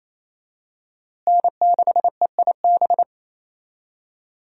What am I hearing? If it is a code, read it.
Morse code, heard as a keyed tone, about 27 wpm, 700 Hz. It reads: N6EIB
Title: USA Call Signs - 5 Characters - Repeated 3x - 28wpm